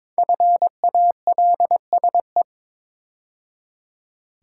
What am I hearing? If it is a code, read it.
Morse code, heard as a keyed tone, about 22 wpm, 700 Hz. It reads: FALSE
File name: 5 Letter Words - First 3 Letters Given - 22wpm